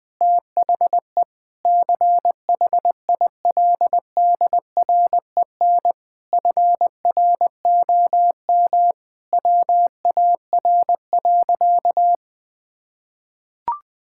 Morse code, heard as a keyed tone, about 20 wpm, 700 Hz. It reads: THE CHILDREN FROM WAR.
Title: Sentences from Top 1000 Words - Review Part 6 - 20wpm